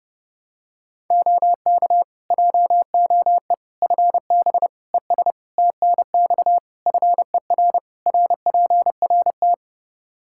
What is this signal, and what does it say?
Morse code, heard as a keyed tone, about 30 wpm, 700 Hz. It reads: OK JOE FB ES TNX FER RPRT
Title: K1USN SST QSO Practice 30wpm